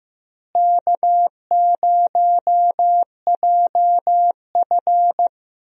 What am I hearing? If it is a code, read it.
Morse code, heard as a keyed tone, about 15 wpm, 700 Hz. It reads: K0JF
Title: USA Call Signs - 4 Characters - Repeated 2x - 15wpm